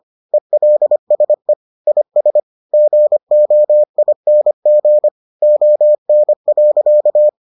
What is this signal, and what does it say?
Morse code, heard as a keyed tone, about 25 wpm, 600 Hz. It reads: ELSE IS GOING ON.